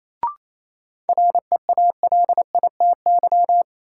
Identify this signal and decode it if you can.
Morse code, heard as a keyed tone, about 27 wpm, 700 Hz. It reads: REALITY
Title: Sentences from Top 1200 Words - New Word Drill Alt - 28wpm